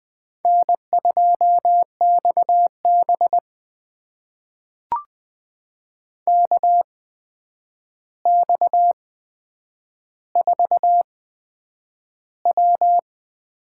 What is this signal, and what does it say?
Morse code, heard as a keyed tone, about 20 wpm, spaced as for 9 wpm, 700 Hz. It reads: N2XB KX4W
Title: USA Call Signs - 4 Characters - 8x Character Spacing - 20wpm